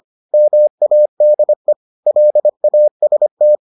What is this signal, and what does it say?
Morse code, heard as a keyed tone, about 25 wpm, 600 Hz. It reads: MADE LAST